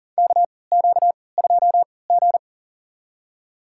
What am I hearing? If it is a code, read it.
Morse code, heard as a keyed tone, about 40 wpm, spaced as for 23 wpm, 700 Hz. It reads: KQ2G
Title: USA Call Signs - 4 Characters - 3x Character Spacing - 40wpm